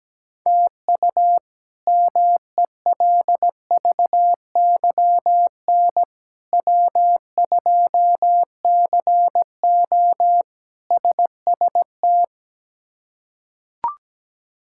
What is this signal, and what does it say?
Morse code, heard as a keyed tone, about 17 wpm, 700 Hz. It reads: TU MELVYN W2CO SST
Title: K1USN SST QSO Practice - 17wpm